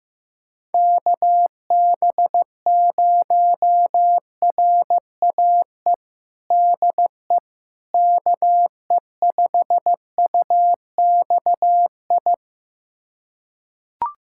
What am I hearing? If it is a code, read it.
Morse code, heard as a keyed tone, about 16 wpm, 700 Hz. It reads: KB0RAE DE KE5UXI
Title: USA Callsign DE Callsign 15wpm